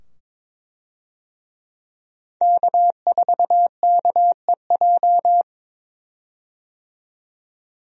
Morse code, heard as a keyed tone, about 22 wpm, 700 Hz. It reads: K4KEJ
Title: USA Call Signs - 5 Characters - 22wpm